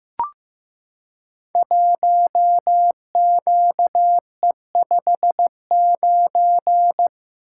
Morse code, heard as a keyed tone, about 15 wpm, 700 Hz. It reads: 1QE59
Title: Code Groups - 5 Letter:Numbers - 15wpm